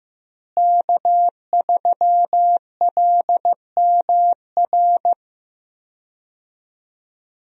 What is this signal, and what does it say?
Morse code, heard as a keyed tone, about 15 wpm, 700 Hz. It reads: K3LMR
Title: USA Call Signs - 5 Characters - Repeated 2x - 15wpm